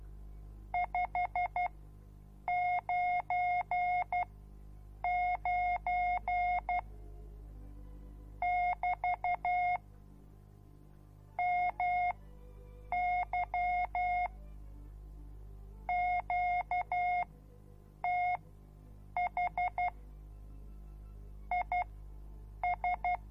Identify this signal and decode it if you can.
Morse code, heard as a keyed tone, about 12 wpm, spaced as for 7 wpm, 720 Hz. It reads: 599 <BT> MY QTH IS